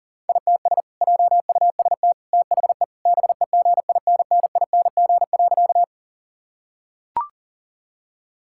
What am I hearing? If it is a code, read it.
Morse code, heard as a keyed tone, about 40 wpm, 700 Hz. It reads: ITS JUST THE BEGINNING.
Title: Sentences from Top 1000 Words - Review Part 5 - 40wpm